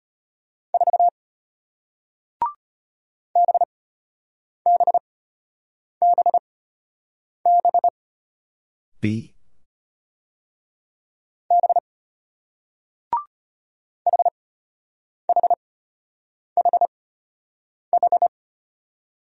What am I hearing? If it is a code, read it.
Morse code, heard as a keyed tone, about 33 wpm, spaced as for 8 wpm, 700 Hz. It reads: 4 BBBB B HHHH